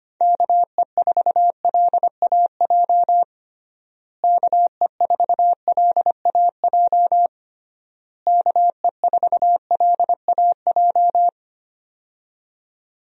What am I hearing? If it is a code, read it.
Morse code, heard as a keyed tone, about 25 wpm, 700 Hz. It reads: KE4LAJ KE4LAJ KE4LAJ
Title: USA Call Signs - 6 Characters - Repeated 3x - 25wpm